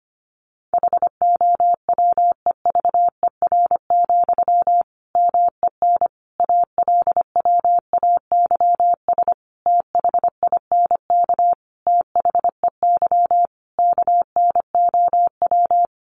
Morse code, heard as a keyed tone, about 25 wpm, 700 Hz. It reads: HOWEVER, MEN ALWAYS THINK THEY KNOW
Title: Sentences from Top 600 Words 25wpm